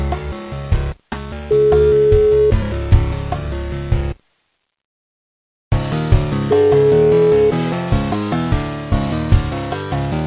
Ringback tone (Europe, 425 Hz).